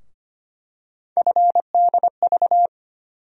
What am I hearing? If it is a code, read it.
Morse code, heard as a keyed tone, about 25 wpm, 700 Hz. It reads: FDV
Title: USA Call Sign Postfixes - 3 Letters 25wpm